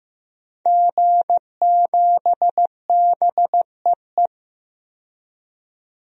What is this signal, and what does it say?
Morse code, heard as a keyed tone, about 15 wpm, 700 Hz. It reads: G7BEE
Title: DXCC-100 Call Signs - 5 Characters - 15wpm